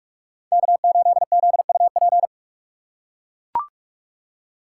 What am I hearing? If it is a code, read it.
Morse code, heard as a keyed tone, about 43 wpm, 700 Hz. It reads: K8ZUP